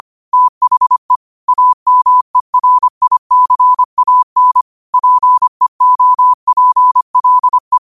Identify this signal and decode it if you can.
Morse code, heard as a keyed tone, about 25 wpm, 1000 Hz. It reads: THE AMERICAN PEOPLE